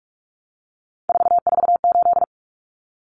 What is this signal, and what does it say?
Morse code, heard as a keyed tone, about 44 wpm, 700 Hz. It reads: 448
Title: Common Cut Numbers - 3-Digit 45wpm